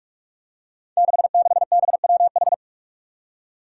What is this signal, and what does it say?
Morse code, heard as a keyed tone, about 44 wpm, 700 Hz. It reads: 66BWH
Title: Five Challenging Letter-Number Combinations 45wpm